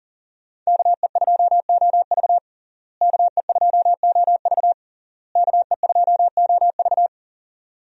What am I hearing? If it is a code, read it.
Morse code, heard as a keyed tone, about 40 wpm, 700 Hz. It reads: KE2OV KE2OV KE2OV
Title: USA Call Signs - 5 Characters - Repeated 3x - 40wpm